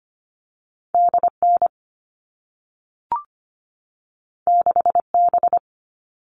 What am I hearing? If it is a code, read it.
Morse code, heard as a keyed tone, about 25 wpm, 700 Hz. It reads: DN 6B
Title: All Cut Numbers - 2-Digit 25wpm